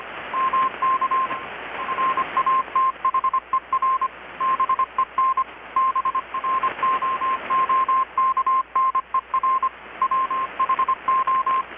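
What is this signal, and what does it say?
Morse code, heard as a keyed tone, about 25 wpm, 1050 Hz, with radio noise. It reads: MK RATHER BEN BROOKNER WHO
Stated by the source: Morse from KPH's Night of Nights XX maritime radiotelegraphy event, received in Chiapas, Mexico about 3,800 km away on a Kenwood R-600 with a grounded 10 m long-wire antenna.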